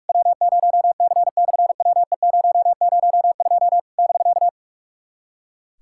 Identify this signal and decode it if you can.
Morse code, heard as a keyed tone, about 44 wpm, 700 Hz. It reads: W0C/WE002 <BK>
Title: SOTA QSO Practice 45wpm